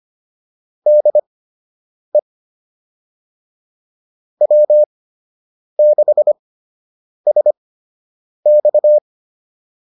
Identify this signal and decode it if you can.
Morse code, heard as a keyed tone, about 25 wpm, spaced as for 8 wpm, 600 Hz. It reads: DE W6SX